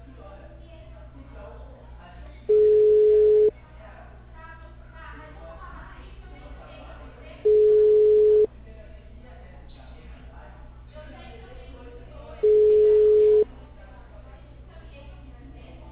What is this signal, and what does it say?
Ringback tone (Europe, 425 Hz), with speech.
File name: out- -2003- - - .31390